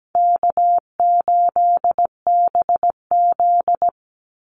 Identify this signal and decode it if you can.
Morse code, heard as a keyed tone, about 17 wpm, 700 Hz. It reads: K8BZ
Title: Easy Call Signs Repeated 2x - 17wpm